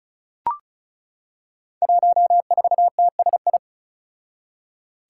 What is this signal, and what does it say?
Morse code, heard as a keyed tone, about 36 wpm, 700 Hz. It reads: 14TSI
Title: Five Challenging Letter-Number Combinations 35wpm